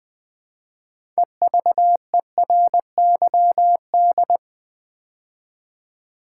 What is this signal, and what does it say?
Morse code, heard as a keyed tone, about 20 wpm, 700 Hz. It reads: EVERYD
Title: Build A Word - 3 Letters At A Time - 8-Letter Words - 20wpm